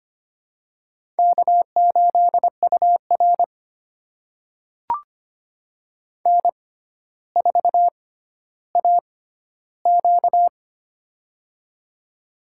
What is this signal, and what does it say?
Morse code, heard as a keyed tone, about 25 wpm, spaced as for 13 wpm, 700 Hz. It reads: K8UR N4AQ
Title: USA Call Signs - 4 Characters - 6x Character Spacing - 25wpm